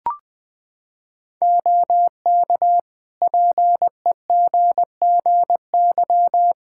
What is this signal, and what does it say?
Morse code, heard as a keyed tone, about 21 wpm, 700 Hz. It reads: OK PEGGY
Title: Practice QSO - Exchange 3 - More Info - 20wpm